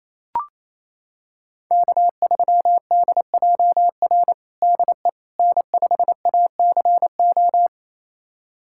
Morse code, heard as a keyed tone, about 28 wpm, 700 Hz. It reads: K3DJR DE N5ACO
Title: USA Callsign DE Callsign - 5 Characters - 28wpm